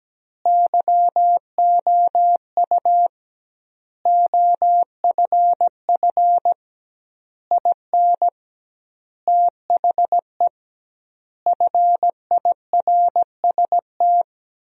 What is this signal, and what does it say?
Morse code, heard as a keyed tone, about 17 wpm, 700 Hz. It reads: YOU OFF IN THE FIRST
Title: 2x Word-Spacing - Sentences from Top 200 Words - 17wpm